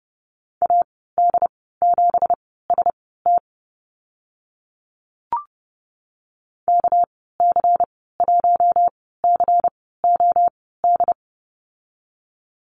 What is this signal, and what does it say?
Morse code, heard as a keyed tone, about 30 wpm, spaced as for 17 wpm, 700 Hz. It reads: AD7ST KC1COD
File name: Lesson 37g - Letter x - Call Signs 30wpm